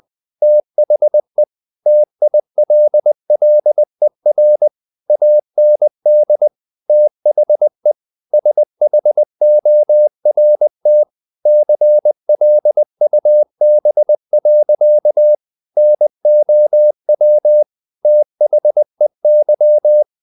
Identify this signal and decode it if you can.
Morse code, heard as a keyed tone, about 20 wpm, 600 Hz. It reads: THE TILLER AND THE SHORT CLUB. NOW THEY
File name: the old man and the sea part 2 20wpm